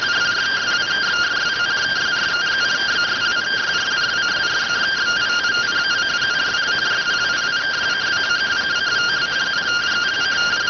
Two-tone teleprinter signal, about 170 Hz shift.